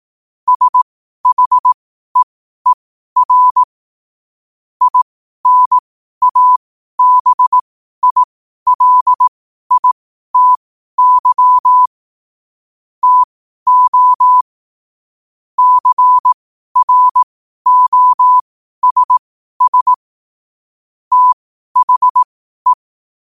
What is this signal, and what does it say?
Morse code, heard as a keyed tone, about 18 wpm, spaced as for 12 wpm, 1000 Hz. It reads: SHEER INABILITY TO CROSS THE
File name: qotdfiles / 12wpm